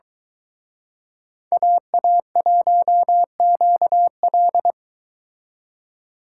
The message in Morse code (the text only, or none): AA1QL